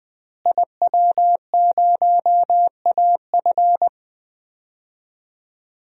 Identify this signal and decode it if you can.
Morse code, heard as a keyed tone, about 21 wpm, 700 Hz. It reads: IW0AF
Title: DXCC-100 Call Signs - 5 Characters - Repeated 2x - 20wpm